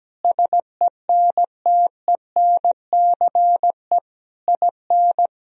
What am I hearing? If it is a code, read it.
Morse code, heard as a keyed tone, about 17 wpm, 700 Hz. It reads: SENTENCE IN
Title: Sentences from Top 1000 Words - Review Part 1 - 17wpm